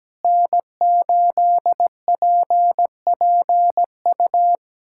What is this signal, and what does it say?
Morse code, heard as a keyed tone, about 17 wpm, 700 Hz. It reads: N8PPU
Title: USA Callsign DE Callsign - 5 Characters - 17wpm